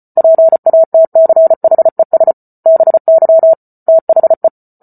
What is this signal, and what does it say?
Morse code, heard as a keyed tone, about 34 wpm, 650 Hz. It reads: PATCHES BY THE